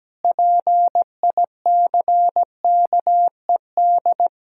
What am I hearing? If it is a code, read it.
Morse code, heard as a keyed tone, about 17 wpm, 700 Hz. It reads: PICKED
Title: Pangram Challenge v2 - 17wpm